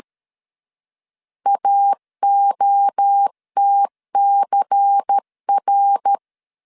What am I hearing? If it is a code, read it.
Morse code, heard as a keyed tone, about 13 wpm, 800 Hz. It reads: AOTCR
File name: LEZIONI / 6 PKX0 / 6 P 12wpm 2033